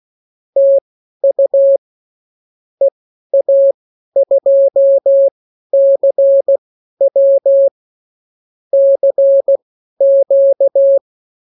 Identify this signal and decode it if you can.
Morse code, heard as a keyed tone, about 16 wpm, spaced as for 12 wpm, 550 Hz. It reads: TU EA2CW CQ
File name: CWconcurso QSO libroCW4 16-8ppm